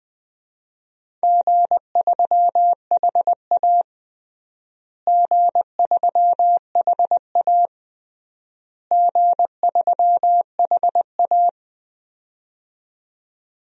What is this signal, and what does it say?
Morse code, heard as a keyed tone, about 20 wpm, 700 Hz. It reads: G3HA G3HA G3HA
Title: DXCC-100 Call Signs - 4 Characters - Repeated 3x - 20wpm